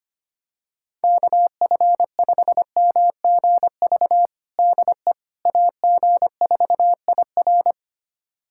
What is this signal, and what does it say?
Morse code, heard as a keyed tone, about 25 wpm, 700 Hz. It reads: KF5MGV DE AG4IR
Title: USA Callsign DE Callsign 25wpm